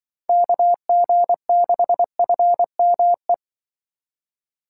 Morse code, heard as a keyed tone, about 24 wpm, 700 Hz. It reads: KG6FME
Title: Speed Racing - USA 2x3 Call Signs 20wpm